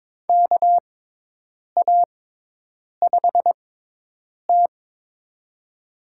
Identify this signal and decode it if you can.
Morse code, heard as a keyed tone, about 22 wpm, spaced as for 8 wpm, 700 Hz. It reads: KA5T
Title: USA Call Signs - 4 Characters - 6x Character Spacing - 22wpm